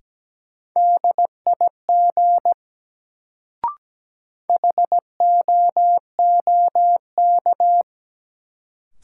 Morse code, heard as a keyed tone, about 17 wpm, 700 Hz. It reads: DIG HOOK
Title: Sentences from Top 1600 Words - New Word Drill - 17wpm